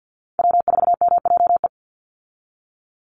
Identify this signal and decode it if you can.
Morse code, heard as a keyed tone, about 50 wpm, 700 Hz. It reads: W4MJE
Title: Call Signs Repeated 2x - 50wpm